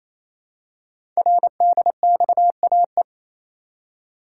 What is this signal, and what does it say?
Morse code, heard as a keyed tone, about 27 wpm, 700 Hz. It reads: RDXAE
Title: Code Groups - 5 Letters - 28wpm